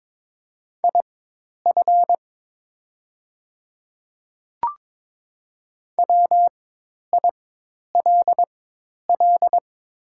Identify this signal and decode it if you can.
Morse code, heard as a keyed tone, about 22 wpm, spaced as for 10 wpm, 700 Hz. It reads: IF WILL
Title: 4x Character Spacing - Top 100 Words - 22wpm